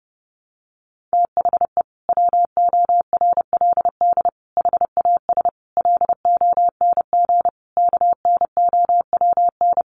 Morse code, heard as a keyed tone, about 30 wpm, 700 Hz. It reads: THE WORLD HAS LONG KNOWN
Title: Sentences from Top 1200 Words 30wpm